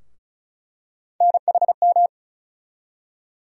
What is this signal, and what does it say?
Morse code, heard as a keyed tone, about 36 wpm, 700 Hz. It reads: NHM
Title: USA Call Sign Postfixes - 3 Letters 35wpm